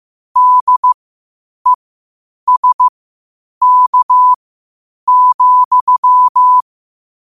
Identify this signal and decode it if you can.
Morse code, heard as a keyed tone, about 15 wpm, spaced as for 8 wpm, 1000 Hz. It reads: DESK,
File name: qotdfiles / 8wpm